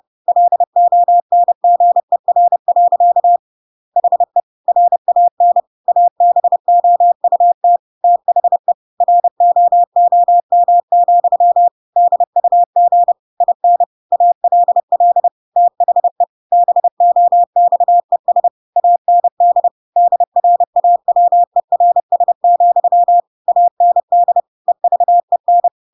Morse code, heard as a keyed tone, about 30 wpm, 700 Hz. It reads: LONGER. HE RAN ABOUT THE ROOM, DUG IN ALL THE BOXES AND DRAWERS, AND EVEN
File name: Pinocchio / Pinocchio ch5 30 30 ik1hge